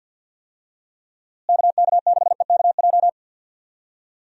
Morse code, heard as a keyed tone, about 50 wpm, 700 Hz. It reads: KK6EKJ
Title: USA Call Signs - 6 Characters - Repeated 3x - 50wpm